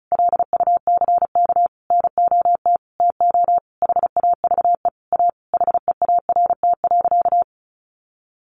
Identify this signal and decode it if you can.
Morse code, heard as a keyed tone, about 34 wpm, 700 Hz. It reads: LUCK NOT TO HAVE A HEART.
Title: Sentences from Top 1500 Words 35wpm